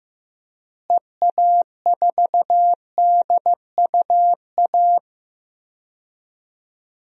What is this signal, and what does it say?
Morse code, heard as a keyed tone, about 15 wpm, 700 Hz. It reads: EA4DUA
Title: DXCC-100 Call Signs - 6 Characters - Repeated 3x - 15wpm